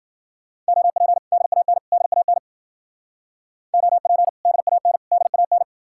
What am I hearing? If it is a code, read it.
Morse code, heard as a keyed tone, about 53 wpm, 700 Hz. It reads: OP DAN DAN OP DAN DAN